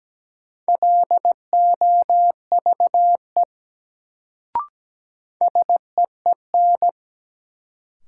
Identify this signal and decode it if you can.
Morse code, heard as a keyed tone, about 17 wpm, 700 Hz. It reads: LOVE SEEN
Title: Sets of 5 Words Encore - New Word Drill - 17wpm